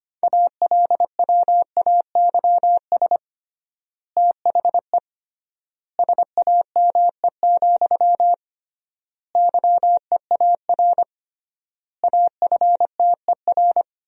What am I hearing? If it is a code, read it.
Morse code, heard as a keyed tone, about 25 wpm, 700 Hz. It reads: ALWAYS THE SAME, YEAR AFTER
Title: 3x Word-Spacing - Sentences from Top 300 Words - 25wpm